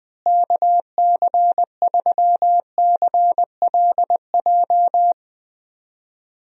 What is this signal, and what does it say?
Morse code, heard as a keyed tone, about 21 wpm, 700 Hz. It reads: KC3CLJ
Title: USA Call Signs - 6 Characters - Repeated 3x - 20wpm